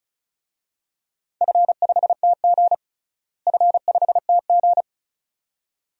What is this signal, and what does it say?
Morse code, heard as a keyed tone, about 36 wpm, 700 Hz. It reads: F5TG F5TG
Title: DXCC-100 Call Signs - 4 Characters - Repeated 2x - 35wpm